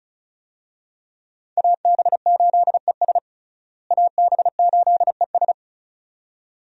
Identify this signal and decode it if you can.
Morse code, heard as a keyed tone, about 36 wpm, 700 Hz. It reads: AB8ES AB8ES
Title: USA Call Signs - 5 Characters - Repeated 2x - 35wpm